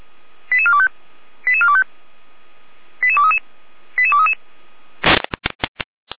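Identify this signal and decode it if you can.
Five-tone selective call (ZVEI), heard as quick runs of five tones, tones of about 70 ms. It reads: 88425 88425 88233 88233
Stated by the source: ZVEI five-tone selective call received on an SDR.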